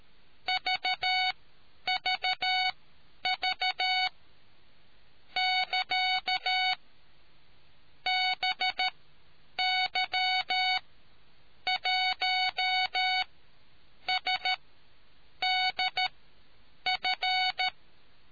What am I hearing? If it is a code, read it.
Morse code, heard as a keyed tone, about 13 wpm, spaced as for 8 wpm, 740 Hz. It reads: VVV <KA> BY1SDF